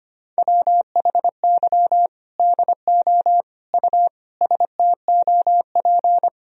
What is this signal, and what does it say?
Morse code, heard as a keyed tone, about 25 wpm, 700 Hz. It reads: WHY DO U STOP
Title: Sentences from Top 300 Words and Abbreviations 25wpm